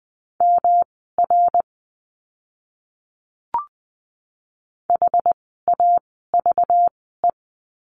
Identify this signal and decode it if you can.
Morse code, heard as a keyed tone, about 20 wpm, spaced as for 15 wpm, 700 Hz. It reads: MR HAVE
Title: 244 Lesson 39f - Letter z - Words 20wpm